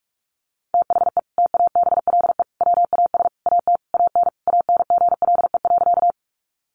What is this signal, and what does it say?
Morse code, heard as a keyed tone, about 44 wpm, 700 Hz. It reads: THE TABLE WAS AT AN ANGLE.
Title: Sentences from Top 1500 Words 45wpm